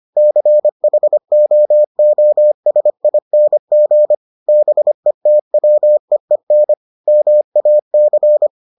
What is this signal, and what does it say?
Morse code, heard as a keyed tone, about 25 wpm, 600 Hz. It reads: CHOOSING BETWEEN MAC